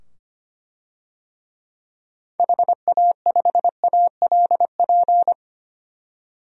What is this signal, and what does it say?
Morse code, heard as a keyed tone, about 25 wpm, 700 Hz. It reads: HA5ALP
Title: DXCC-100 Call Signs - 6 Characters - 25wpm